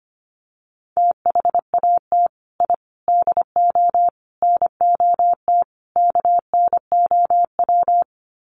Morse code, heard as a keyed tone, about 25 wpm, 700 Hz. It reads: THAT I DO NOT KNOW
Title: Sets of 5 Words Encore2 25wpm